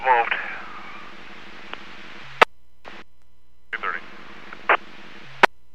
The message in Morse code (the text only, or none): none